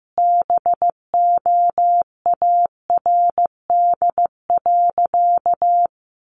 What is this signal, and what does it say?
Morse code, heard as a keyed tone, about 15 wpm, 700 Hz. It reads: BOARD.
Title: Additional Easy Phrases 15wpm